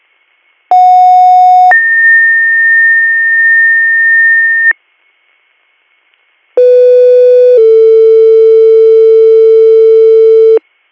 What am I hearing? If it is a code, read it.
Two-tone page, heard as a one-second tone, then a three-second tone, about 730 then 1800 Hz, about 500 then 450 Hz. It is heard from 0.7 to 10.6 s.